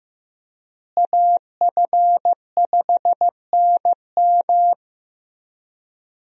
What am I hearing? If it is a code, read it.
Morse code, heard as a keyed tone, about 15 wpm, 700 Hz. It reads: AF5NM
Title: USA Call Signs - 5 Characters - Repeated 2x - 15wpm